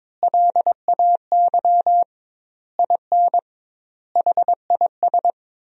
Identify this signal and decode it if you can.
Morse code, heard as a keyed tone, about 22 wpm, 700 Hz. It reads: LAY IN HIS
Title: 2x Word-Spacing - Sentences from Top 500 Words - 22wpm